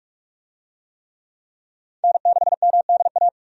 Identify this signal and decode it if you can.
Morse code, heard as a keyed tone, about 47 wpm, 700 Hz. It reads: N6MDA
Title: USA Call Signs - 5 Characters - 45wpm